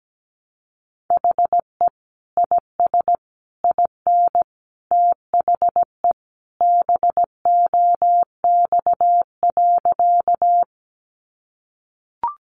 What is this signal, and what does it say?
Morse code, heard as a keyed tone, about 17 wpm, 700 Hz. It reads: HE IS IN THE BOX.
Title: Sentences from Top 500 Words 17wpm